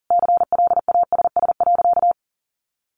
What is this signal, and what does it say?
Morse code, heard as a keyed tone, about 40 wpm, 700 Hz. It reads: CLASS.
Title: Sentences from Top 1400 Words 40wpm